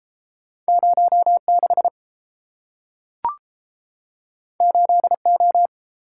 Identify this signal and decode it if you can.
Morse code, heard as a keyed tone, about 30 wpm, 700 Hz. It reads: 06 8O